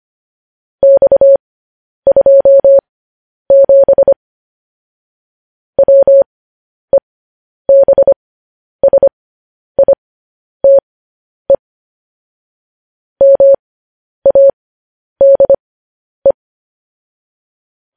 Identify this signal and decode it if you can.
Morse code, heard as a keyed tone, about 25 wpm, spaced as for 10 wpm, 550 Hz. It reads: X27 WEBSITE MADE
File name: podcast / News headlines-1x-25x10